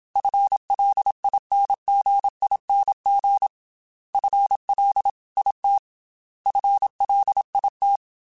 Morse code, heard as a keyed tone, about 27 wpm, 800 Hz. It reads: FLINGING FLIT FLIT
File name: FL / FL mp3s / FL Prefixes 3x2 sp @25